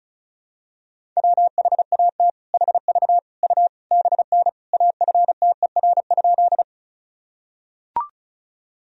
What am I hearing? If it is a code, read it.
Morse code, heard as a keyed tone, about 34 wpm, 700 Hz. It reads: WHAT HV U BN AFTER?
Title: Sentences from Top 200 Words and Abbreviations 35wpm